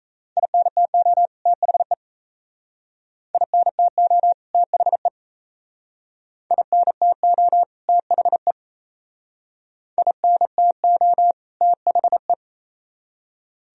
Morse code, heard as a keyed tone, about 35 wpm, 700 Hz. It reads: INTO THE INTO THE INTO THE INTO THE
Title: Speed-Racing - Sets of 2 Words 28wpm